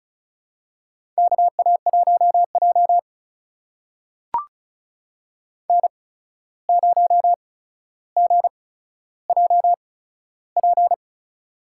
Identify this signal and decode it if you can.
Morse code, heard as a keyed tone, about 34 wpm, spaced as for 14 wpm, 700 Hz. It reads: KA1J N0GJP
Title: USA Call Signs - 8x Character Spacing - 35wpm